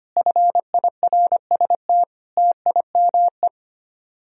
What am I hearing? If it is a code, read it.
Morse code, heard as a keyed tone, about 31 wpm, 700 Hz. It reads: FIRST TIME
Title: Speed-Racing - Sets of 4 Words 25wpm